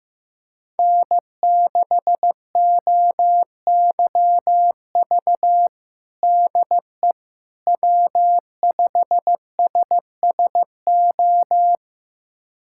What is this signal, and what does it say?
Morse code, heard as a keyed tone, about 16 wpm, 700 Hz. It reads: N6OYV DE W5SSO
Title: USA Callsign DE Callsign - 5 Characters - 15wpm